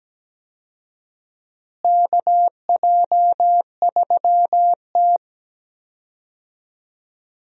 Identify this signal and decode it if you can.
Morse code, heard as a keyed tone, about 17 wpm, 700 Hz. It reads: KJ3T
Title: USA Call Signs - 4 Characters - Repeated 3x - 17wpm